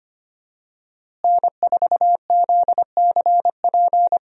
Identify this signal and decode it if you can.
Morse code, heard as a keyed tone, about 25 wpm, 700 Hz. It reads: N4ZCP
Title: USA Call Signs - 5 Characters - Repeated 2x - 25wpm